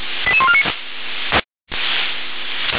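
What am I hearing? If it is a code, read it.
Five-tone selective call (ZVEI), heard as quick runs of five tones, tones of about 70 ms. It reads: 88149
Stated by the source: ZVEI five-tone selective call received on SDR.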